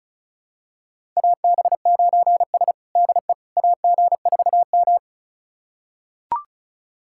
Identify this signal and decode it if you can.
Morse code, heard as a keyed tone, about 36 wpm, 700 Hz. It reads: AB9S DE AG4M
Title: USA Callsign DE Callsign - 4 Characters - 35wpm